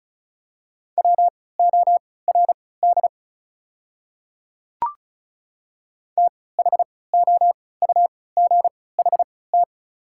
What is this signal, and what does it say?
Morse code, heard as a keyed tone, about 36 wpm, spaced as for 20 wpm, 700 Hz. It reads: WORD THOUGHT
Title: 3x Character Spacing - Top 300 Words - 35wpm